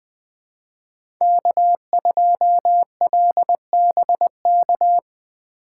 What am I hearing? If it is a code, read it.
Morse code, heard as a keyed tone, about 20 wpm, 700 Hz. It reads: K2LBK